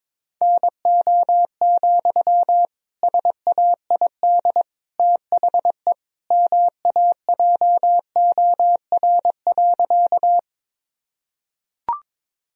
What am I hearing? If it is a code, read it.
Morse code, heard as a keyed tone, about 22 wpm, 700 Hz. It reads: NO, SAID THE MAJOR.
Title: Sentences from Top 1000 Words - Review Part 4 - 22wpm